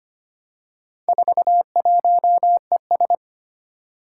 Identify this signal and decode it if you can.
Morse code, heard as a keyed tone, about 25 wpm, 700 Hz. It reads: 41ES
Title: Four Challenging Letter-Number Combinations 25wpm